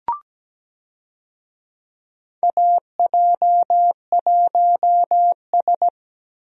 Morse code, heard as a keyed tone, about 17 wpm, 700 Hz. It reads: AJ1S
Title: USA Call Signs - 4 Characters - Repeated 2x - 17wpm